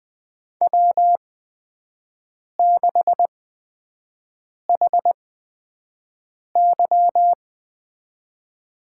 Morse code, heard as a keyed tone, about 20 wpm, spaced as for 5 wpm, 700 Hz. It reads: W6HY
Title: USA Call Signs - 4 Characters - 8x Character Spacing - 20wpm